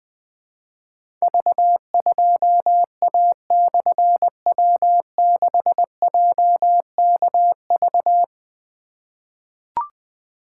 Morse code, heard as a keyed tone, about 21 wpm, 700 Hz. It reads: V2A/W6JKV
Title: Difficult Call Signs 20wpm